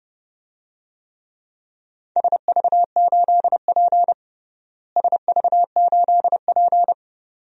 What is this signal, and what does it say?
Morse code, heard as a keyed tone, about 30 wpm, 700 Hz. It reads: SV8P SV8P
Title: DXCC-100 Call Signs - 4 Characters - Repeated 2x - 30wpm